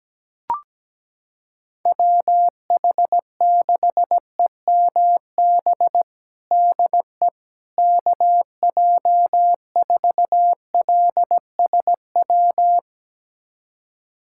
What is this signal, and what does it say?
Morse code, heard as a keyed tone, about 17 wpm, 700 Hz. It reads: WH6EMB DE KJ4LSW
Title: USA Callsign DE Callsign 17wpm